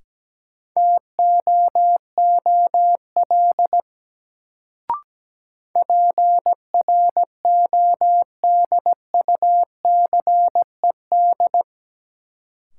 Morse code, heard as a keyed tone, about 17 wpm, 700 Hz. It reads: TOOL PRODUCED
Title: Sentences from Top 1000 Words - Review Part 6 - New Word Drill - 17wpm